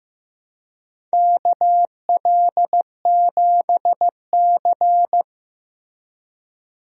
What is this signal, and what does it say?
Morse code, heard as a keyed tone, about 15 wpm, 700 Hz. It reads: KL7C
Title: DXCC-100 Call Signs - 4 Characters - Repeated 3x - 15wpm